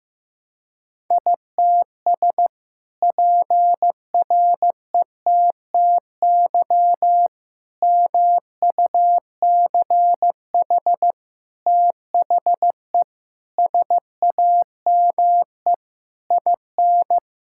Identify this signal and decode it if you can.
Morse code, heard as a keyed tone, about 14 wpm, 700 Hz. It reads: ITS PRETTY MUCH THE SAME IN
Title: Sentences from Top 1000 Words - Review Part 6 - 15wpm